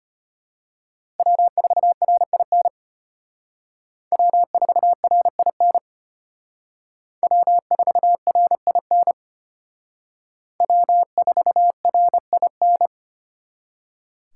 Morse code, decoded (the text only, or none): W4RIN W4RIN W4RIN W4RIN